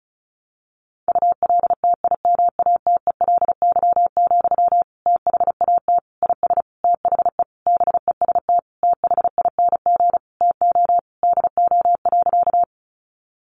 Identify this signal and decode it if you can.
Morse code, heard as a keyed tone, about 34 wpm, 700 Hz. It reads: ULTIMATELY, THAT IS THE BEST THING TO DO.
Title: Sentences from Top 1200 Words 35wpm